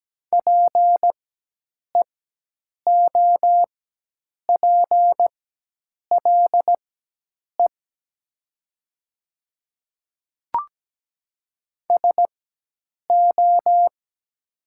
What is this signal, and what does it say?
Morse code, heard as a keyed tone, about 17 wpm, spaced as for 8 wpm, 700 Hz. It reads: PEOPLE SO
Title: 4x Character Spacing - Top 100 Words - 17wpm